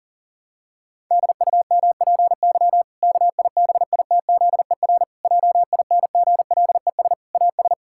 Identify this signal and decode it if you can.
Morse code, heard as a keyed tone, about 40 wpm, 700 Hz. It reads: DUMPY KIBITZER JINGLES AS
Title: Pangram Challenge v2 - 40wpm